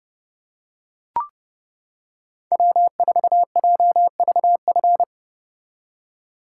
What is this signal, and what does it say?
Morse code, heard as a keyed tone, about 29 wpm, 700 Hz. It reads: W4JVF